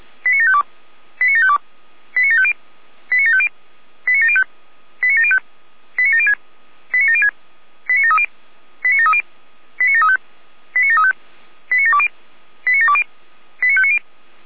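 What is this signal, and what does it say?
Five-tone selective call (ZVEI), heard as quick runs of five tones, tones of about 70 ms. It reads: 89752 89752 89750 89750 89705 89705 89706 89706 89730 89730 89735 89735 89720 89720 89590
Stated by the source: ZVEI five-tone selective call received via SDR.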